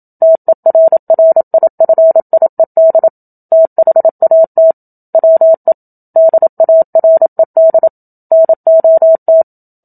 Morse code, heard as a keyed tone, about 27 wpm, 650 Hz. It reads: TERRIFIED THAT WE DARED NOT
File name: Book Two / 01-Chapter 1 - 27 WPM